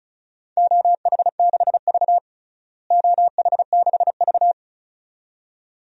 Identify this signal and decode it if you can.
Morse code, heard as a keyed tone, about 34 wpm, 700 Hz. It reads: OH6V OH6V
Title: DXCC-100 Call Signs - 4 Characters - Repeated 2x - 35wpm